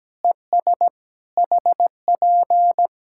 Morse code, heard as a keyed tone, about 17 wpm, 700 Hz. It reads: ES HP
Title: Practice QSO - Exchange 4 - Ending - 17wpm